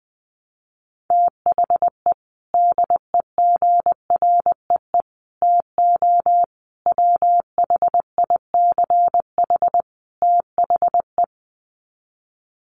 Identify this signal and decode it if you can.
Morse code, heard as a keyed tone, about 20 wpm, 700 Hz. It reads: THE DEGREE TO WHICH THE
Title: Sets of 5 Words Encore2 20wpm